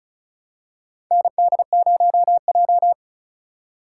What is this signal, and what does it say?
Morse code, heard as a keyed tone, about 36 wpm, 700 Hz. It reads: ND0J
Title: USA Call Signs - 4 Characters - 4x Character Spacing - 35wpm